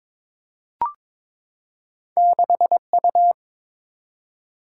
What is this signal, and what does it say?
Morse code, heard as a keyed tone, about 22 wpm, 700 Hz. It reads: 6U
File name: practice / 22wpm / Two Letter-Number Combinations - 22wpm